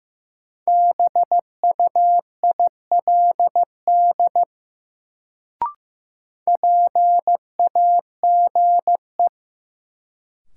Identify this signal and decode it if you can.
Morse code, heard as a keyed tone, about 15 wpm, 700 Hz. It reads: BUILD PAGE
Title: Sentences from Top 1000 Words - Review Part 1 - New Word Drill - 15wpm